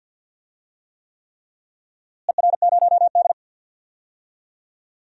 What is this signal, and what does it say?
Morse code, heard as a keyed tone, about 46 wpm, 700 Hz. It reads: ER0D
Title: DXCC-100 Call Signs - 4 Characters - Repeated 3x - 50wpm